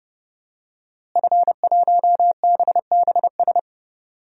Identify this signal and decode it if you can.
Morse code, heard as a keyed tone, about 30 wpm, 700 Hz. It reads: F1BBS
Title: DXCC-100 Call Signs - 5 Characters - Repeated 3x - 30wpm